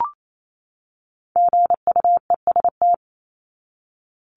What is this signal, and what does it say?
Morse code, heard as a keyed tone, about 28 wpm, 700 Hz. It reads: GUEST
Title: Top 1501-1750 Words 28wpm